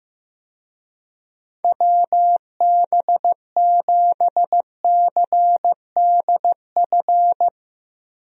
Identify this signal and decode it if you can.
Morse code, heard as a keyed tone, about 16 wpm, 700 Hz. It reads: WB7CDF
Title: USA Call Signs - 6 Characters - Repeated 3x - 15wpm